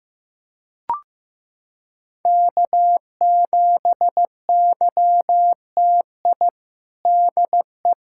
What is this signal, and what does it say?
Morse code, heard as a keyed tone, about 15 wpm, 700 Hz. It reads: K7YTI DE
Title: USA Callsign DE Callsign - 5 Characters - 15wpm